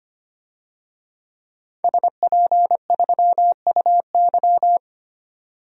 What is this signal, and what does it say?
Morse code, heard as a keyed tone, about 25 wpm, 700 Hz. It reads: SP3UY
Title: DXCC-100 Call Signs - 5 Characters - 25wpm